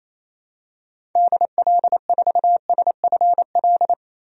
Morse code, heard as a keyed tone, about 28 wpm, 700 Hz. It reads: DL4SFL
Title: DXCC-100 Call Signs - 6 Characters - 28wpm